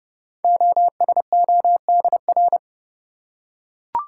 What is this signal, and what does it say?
Morse code, heard as a keyed tone, about 30 wpm, 700 Hz. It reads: OSODR
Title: Code Groups - 5 Letters - 30wpm